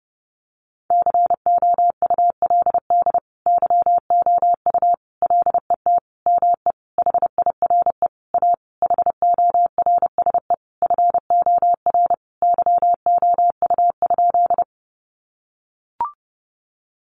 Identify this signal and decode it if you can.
Morse code, heard as a keyed tone, about 28 wpm, 700 Hz. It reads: COULD YOU LET ME HIRE A HORSE FOR YOU?
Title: Sentences from Top 1400 Words 30wpm